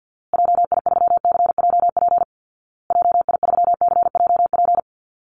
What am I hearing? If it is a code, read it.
Morse code, heard as a keyed tone, about 50 wpm, 700 Hz. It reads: JI3CJP JI3CJP
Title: Call Signs Repeated 2x - 50wpm